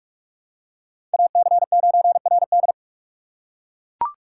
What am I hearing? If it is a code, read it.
Morse code, heard as a keyed tone, about 43 wpm, 700 Hz. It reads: AC9RD